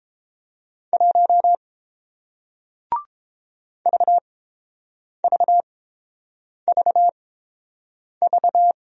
Morse code, heard as a keyed tone, about 30 wpm, spaced as for 7 wpm, 700 Hz. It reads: 1 VVVV